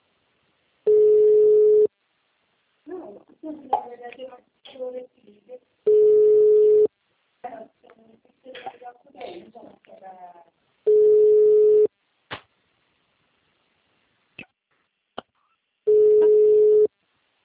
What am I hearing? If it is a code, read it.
Ringback tone (Europe, 425 Hz), with speech.